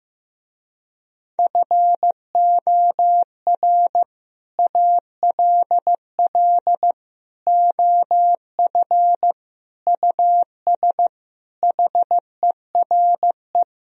Morse code, heard as a keyed tone, about 16 wpm, 700 Hz. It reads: FOR ALL OF US HERE
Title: Speed-Racing - Sets of 5 Words 15wpm